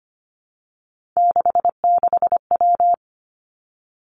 Morse code, heard as a keyed tone, about 25 wpm, 700 Hz. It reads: 66W
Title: Three Challenging Letter-Number Combinations 25wpm